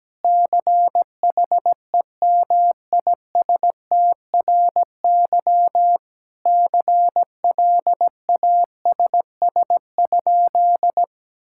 Morse code, heard as a keyed tone, about 17 wpm, 700 Hz. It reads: CHEMISTRY CLASS?
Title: Sentences from Top 1000 Words - Review Part 6 - 17wpm